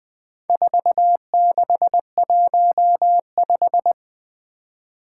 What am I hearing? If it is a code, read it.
Morse code, heard as a keyed tone, about 20 wpm, 700 Hz. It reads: 4615
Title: Four Challenging Letter-Number Combinations 20wpm